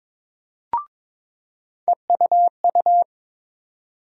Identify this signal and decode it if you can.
Morse code, heard as a keyed tone, about 22 wpm, 700 Hz. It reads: EUU
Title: USA Call Sign Postfixes - 3 Letters 22wpm